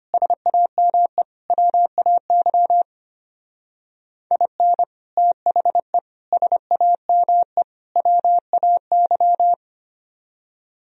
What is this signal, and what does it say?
Morse code, heard as a keyed tone, about 30 wpm, 700 Hz. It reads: SAME WAY IN THE SAME WAY
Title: Speed-Racing - Sets of 4 Words 25wpm